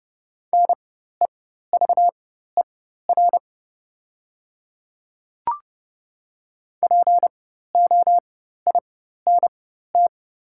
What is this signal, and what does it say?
Morse code, heard as a keyed tone, about 30 wpm, spaced as for 14 wpm, 700 Hz. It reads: NEVER POINT
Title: 4x Character Spacing - Top 300 Words - 30wpm